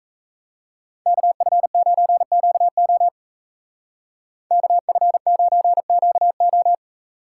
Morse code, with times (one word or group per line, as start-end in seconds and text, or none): 1.1-3.1 KF9QO
4.5-6.7 KF9QO